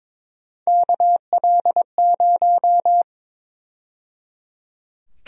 Morse code, heard as a keyed tone, about 22 wpm, 700 Hz. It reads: KL0
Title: USA Call Sign Prefixes - 22wpm